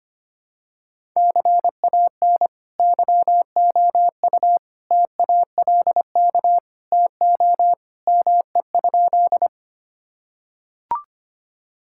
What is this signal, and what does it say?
Morse code, heard as a keyed tone, about 25 wpm, 700 Hz. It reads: CAN YOU TALK TO ME?
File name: practice / 25wpm / Sentences with 2-Word Phrases - 25wpm